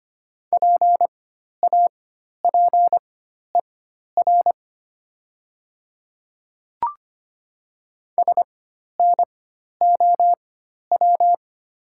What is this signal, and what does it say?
Morse code, heard as a keyed tone, about 25 wpm, spaced as for 12 wpm, 700 Hz. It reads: PAPER SNOW
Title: 4x Character Spacing - Top 500 Words - 25wpm